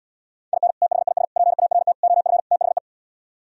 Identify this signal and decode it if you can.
Morse code, heard as a keyed tone, about 77 wpm, 700 Hz. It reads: IN THIS PART OF THE